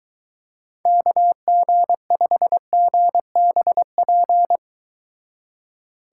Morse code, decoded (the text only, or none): KG5GBP